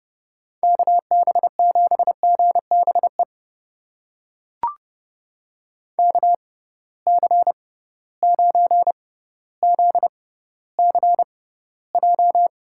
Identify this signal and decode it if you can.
Morse code, heard as a keyed tone, about 32 wpm, spaced as for 16 wpm, 700 Hz. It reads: KB7GBE KC9ZCJ
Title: USA Call Signs - 6 Characters - 6x Character Spacing - 30wpm